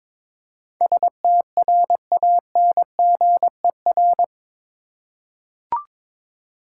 Morse code, heard as a keyed tone, about 22 wpm, 700 Hz. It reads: STRANGER
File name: practice / 22wpm / Eight Letter Words - 22wpm